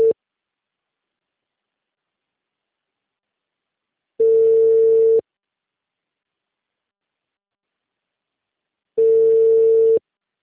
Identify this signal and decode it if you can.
Ringback tone (China, 450 Hz).